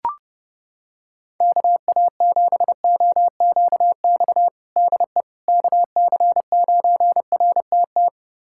Morse code, heard as a keyed tone, about 30 wpm, 700 Hz. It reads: KA7OQX DE KC9RTT
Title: USA Callsign DE Callsign - 6 Characters - 30wpm